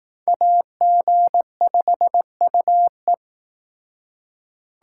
Morse code, heard as a keyed tone, about 19 wpm, 700 Hz. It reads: AG5UE
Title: Speed Racing - USA 2x2 Call Signs 15wpm